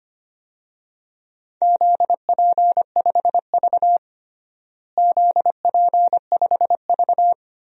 Morse code, heard as a keyed tone, about 25 wpm, 700 Hz. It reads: ZP5V ZP5V
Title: DXCC-100 Call Signs - 4 Characters - Repeated 3x - 25wpm